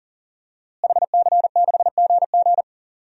Morse code, heard as a keyed tone, about 39 wpm, 700 Hz. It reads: HC6GG
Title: DXCC-100 Call Signs - 5 Characters - Repeated 2x - 40wpm